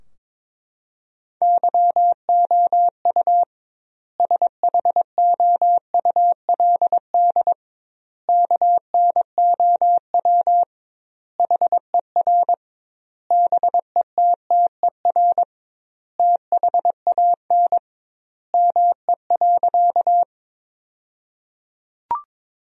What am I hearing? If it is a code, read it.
Morse code, heard as a keyed tone, about 22 wpm, 700 Hz. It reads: YOU SHOULD KNOW HER BETTER THAN ME.
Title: 2x Word-Spacing - Sentences from Top 400 Words - 22wpm